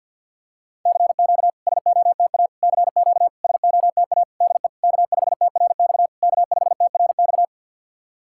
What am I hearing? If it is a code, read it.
Morse code, heard as a keyed tone, about 53 wpm, 690 Hz. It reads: CQ SOTA CQ SOTA DE K5TRX K5TRX
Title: SOTA QSO Practice 50wpm